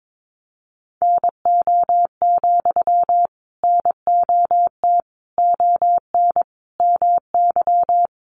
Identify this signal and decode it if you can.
Morse code, heard as a keyed tone, about 22 wpm, 700 Hz. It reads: NO, NOT ON MY
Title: Sentences from Top 900 Words 22wpm